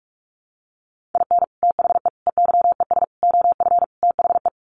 Morse code, heard as a keyed tone, about 44 wpm, 700 Hz. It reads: IN THE EYES OF THE
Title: Sets of 5 Words 45wpm